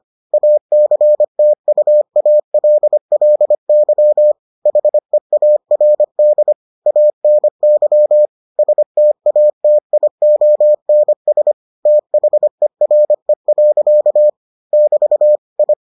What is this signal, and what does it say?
Morse code, heard as a keyed tone, about 25 wpm, 600 Hz. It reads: ACTUALLY HEARD ANY STATIONS THERE. <BT> I